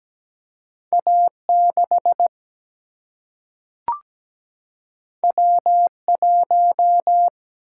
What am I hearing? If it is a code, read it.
Morse code, heard as a keyed tone, about 17 wpm, 700 Hz. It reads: A 6 W 1
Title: Single-Letter Single-Number 17wpm